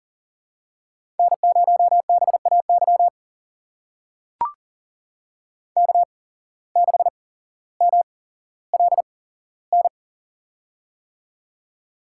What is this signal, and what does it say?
Morse code, heard as a keyed tone, about 43 wpm, spaced as for 17 wpm, 700 Hz. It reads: N0BAY K6MLN
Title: USA Call Signs - 5 Characters - 8x Character Spacing - 40wpm